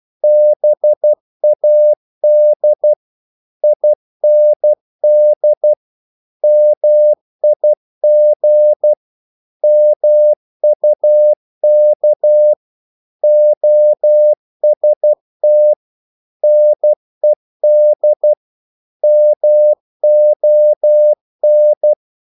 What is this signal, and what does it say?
Morse code, heard as a keyed tone, about 12 wpm, 600 Hz. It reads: BAD IND MIG MUK OST NED MON
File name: Korte ord DK 12wpm